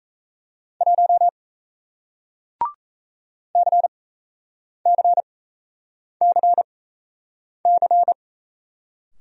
Morse code, heard as a keyed tone, about 38 wpm, spaced as for 8 wpm, 700 Hz. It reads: 1 CCCC